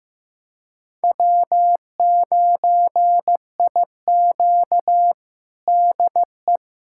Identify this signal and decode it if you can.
Morse code, heard as a keyed tone, about 15 wpm, 700 Hz. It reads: W9IQ DE
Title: USA Callsign DE Callsign - 4 Characters - 15wpm